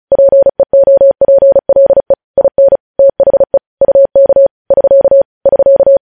Morse code, heard as a keyed tone, about 36 wpm, 550 Hz. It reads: PEOPLE IN THE UK <SK> <SK>
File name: podcast / News headlines-1x-35x35